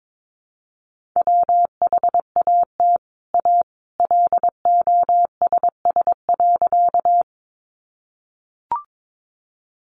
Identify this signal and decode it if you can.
Morse code, heard as a keyed tone, about 22 wpm, 700 Hz. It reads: WHAT A LOSS.
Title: Sentences from Top 1400 Words 22wpm